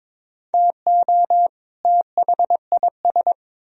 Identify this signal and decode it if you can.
Morse code, heard as a keyed tone, about 26 wpm, 700 Hz. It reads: TO THIS
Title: Speed-Racing - Sets of 2 Words 22wpm